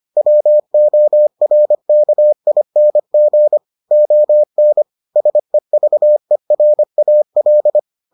Morse code, heard as a keyed tone, about 25 wpm, 600 Hz. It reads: WORKING ON SEVERAL